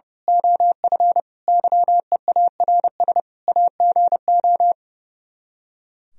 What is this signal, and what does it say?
Morse code, heard as a keyed tone, about 28 wpm, 700 Hz. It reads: OF YEARS AGO
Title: Speed-Racing - Sets of 5 Words 30wpm